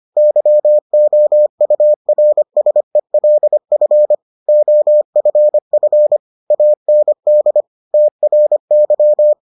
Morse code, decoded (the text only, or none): YOURSELF OFF AND TRY